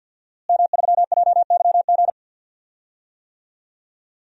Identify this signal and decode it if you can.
Morse code, heard as a keyed tone, about 52 wpm, 700 Hz. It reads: M3JYG
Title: DXCC-100 Call Signs - 5 Characters - Repeated 3x - 50wpm